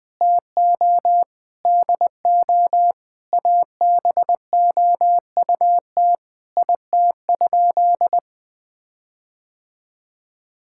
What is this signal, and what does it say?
Morse code, heard as a keyed tone, about 21 wpm, 700 Hz. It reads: TO DO ABOUT IT?
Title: Sentences with 6-Word Phrases - Spoken Beginning - 20wpm